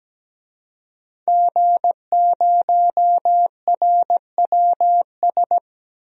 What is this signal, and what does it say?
Morse code, heard as a keyed tone, about 17 wpm, 700 Hz. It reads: G0RWS
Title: DXCC-100 Call Signs - 5 Characters - Repeated 2x - 17wpm